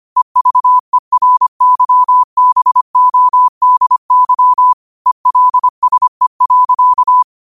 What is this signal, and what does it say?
Morse code, heard as a keyed tone, about 25 wpm, 1000 Hz. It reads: EVERYBODY ELSE.
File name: qotdfiles / 25wpm